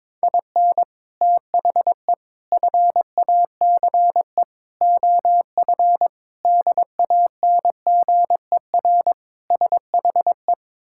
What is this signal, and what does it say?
Morse code, heard as a keyed tone, about 22 wpm, 700 Hz. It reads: IN THE FACE OF DANGER SHE
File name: practice / 22wpm / Sentences with 4-Word Phrases - 22wpm